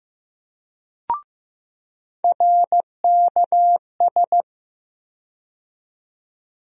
Morse code, heard as a keyed tone, about 15 wpm, 700 Hz. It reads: RKS
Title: USA Call Sign Postfixes - 3 Letters 15wpm